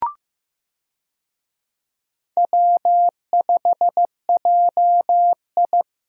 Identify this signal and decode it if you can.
Morse code, heard as a keyed tone, about 14 wpm, 700 Hz. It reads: W5JI
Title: USA Call Signs - 4 Characters - Repeated 3x - 15wpm